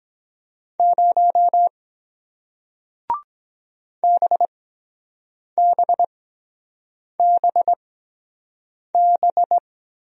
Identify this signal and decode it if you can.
Morse code, heard as a keyed tone, about 23 wpm, spaced as for 7 wpm, 700 Hz. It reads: 0 BBBB